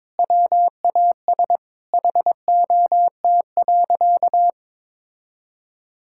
Morse code, heard as a keyed tone, about 23 wpm, 700 Hz. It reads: WAS HOT.
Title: Sentences from Top 1000 Words - Review Part 5 - 22wpm